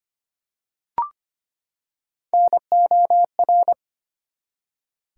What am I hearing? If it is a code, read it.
Morse code, heard as a keyed tone, about 25 wpm, 700 Hz. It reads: NOR
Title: Three Letter Words - 25wpm